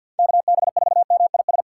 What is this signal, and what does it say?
Morse code, heard as a keyed tone, about 51 wpm, 700 Hz. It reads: KB4MIS
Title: USA Call Signs - 3x Character Spacing - 50wpm